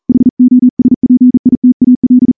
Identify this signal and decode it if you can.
Morse code, heard as a keyed tone, about 38 wpm, 250 Hz. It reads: HOSPITAL
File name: morse-training-40WPM